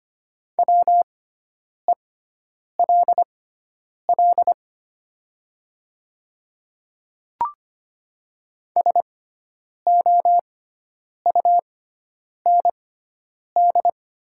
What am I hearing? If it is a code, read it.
Morse code, heard as a keyed tone, about 25 wpm, spaced as for 9 wpm, 700 Hz. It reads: WELL SOUND
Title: 6x Character Spacing - Top 200 Words - 25wpm